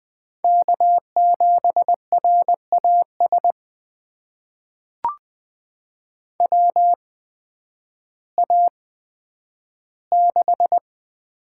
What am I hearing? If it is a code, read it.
Morse code, heard as a keyed tone, about 21 wpm, spaced as for 11 wpm, 700 Hz. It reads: K7RAS WA6
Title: USA Call Signs - 8x Character Spacing - 20wpm